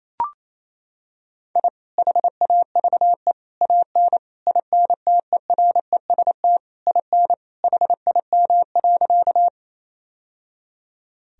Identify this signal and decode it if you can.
Morse code, heard as a keyed tone, about 28 wpm, 700 Hz. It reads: I HAVE AN INTEREST IN HIM.
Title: Sentences from Top 1000 Words - Review Part 2 - 28wpm